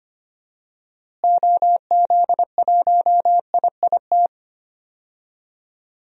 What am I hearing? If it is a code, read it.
Morse code, heard as a keyed tone, about 25 wpm, 700 Hz. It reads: OZ1IIT
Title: DXCC-100 Call Signs - 6 Characters - Repeated 3x - 25wpm